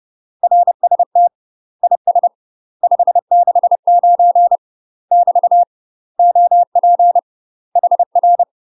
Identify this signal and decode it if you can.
Morse code, heard as a keyed tone, about 30 wpm, 700 Hz. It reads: RST IS 569 <BT> OP HR
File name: UK MASTER QSOs TRAINING / 30WPM / UK MASTER QSOs 0038